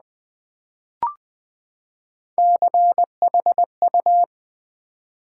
Morse code, heard as a keyed tone, about 20 wpm, 700 Hz. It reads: CHU
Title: Build A Word - 3 Letters At A Time - 6-Letter Words - 20wpm